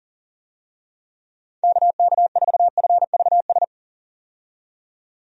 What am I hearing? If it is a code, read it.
Morse code, heard as a keyed tone, about 40 wpm, 700 Hz. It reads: KK4FVS
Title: USA Call Signs - 6 Characters - 40wpm